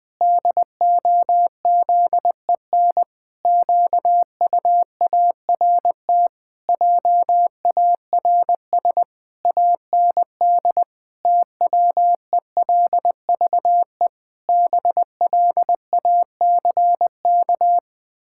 Morse code, heard as a keyed tone, about 21 wpm, 700 Hz. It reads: DOZEN QUART JARS AND TWELVE BLACK
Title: Pangram Challenge v2 - 20wpm